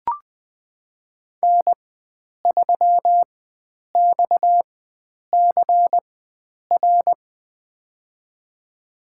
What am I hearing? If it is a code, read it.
Morse code, heard as a keyed tone, about 20 wpm, spaced as for 9 wpm, 700 Hz. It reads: N3XCR
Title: USA Call Signs - 5 Characters - 4x Character Spacing - 20wpm